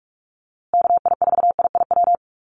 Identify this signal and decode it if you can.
Morse code, heard as a keyed tone, about 44 wpm, 700 Hz. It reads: KI4IIW
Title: USA Call Signs Repeated 3x 45wpm